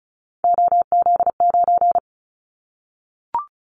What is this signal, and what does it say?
Morse code, heard as a keyed tone, about 36 wpm, 700 Hz. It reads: OZ9
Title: Callsign Prefixes for Top 50 DXCC - 35wpm